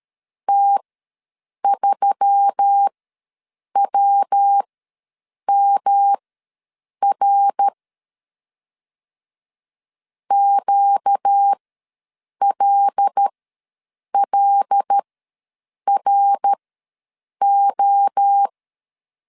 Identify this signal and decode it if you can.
Morse code, heard as a keyed tone, about 13 wpm, spaced as for 7 wpm, 800 Hz. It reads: T3WMR QLLRO
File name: LEZIONI / 5 LBVC / 5 L ws 12wpm 2958